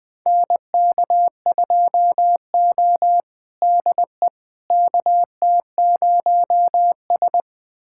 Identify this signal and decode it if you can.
Morse code, heard as a keyed tone, about 20 wpm, 700 Hz. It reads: NK2O DE KT0S
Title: USA Callsign DE Callsign - 4 Characters - 20wpm